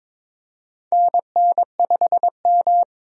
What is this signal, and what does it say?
Morse code, heard as a keyed tone, about 22 wpm, 700 Hz. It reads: NN5M
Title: USA Call Signs - 4 Characters - 22wpm